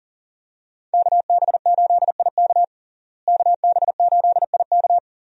Morse code, heard as a keyed tone, about 40 wpm, 700 Hz. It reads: KB8IK KB8IK
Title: USA Call Signs - 5 Characters - Repeated 3x - 40wpm